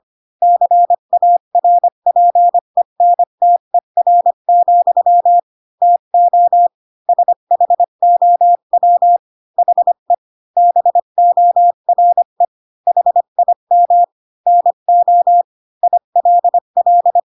Morse code, heard as a keyed tone, about 25 wpm, 700 Hz. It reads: CARPENTER, TO SHOW HE BORE HIM NO ILL